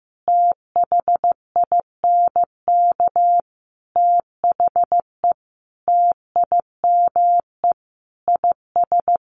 Morse code, heard as a keyed tone, about 14 wpm, 700 Hz. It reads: THINK THE TIME IS
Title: Sentences from Top 1400 Words 15wpm